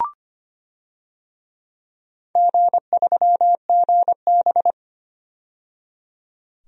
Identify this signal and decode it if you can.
Morse code, heard as a keyed tone, about 25 wpm, 700 Hz. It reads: G3GB
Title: DXCC-100 Call Signs - 4 Characters - 25wpm